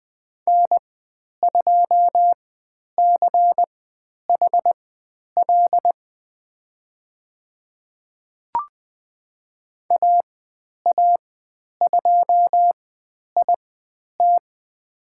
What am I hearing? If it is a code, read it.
Morse code, heard as a keyed tone, about 20 wpm, spaced as for 10 wpm, 700 Hz. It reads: N2CHL AA2IT
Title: 095 Lesson 16g - Letter c - Call Signs 20wpm